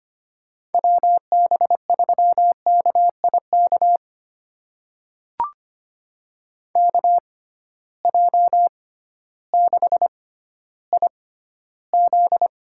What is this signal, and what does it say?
Morse code, heard as a keyed tone, about 25 wpm, spaced as for 13 wpm, 700 Hz. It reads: WB3KIK KJ6IZ
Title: USA Call Signs - 6 Characters - 6x Character Spacing - 25wpm